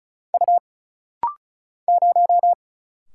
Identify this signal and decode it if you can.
Morse code, heard as a keyed tone, about 34 wpm, 700 Hz. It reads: U 0
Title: Single Letter-Number Alt - Rapid-Fire 35wpm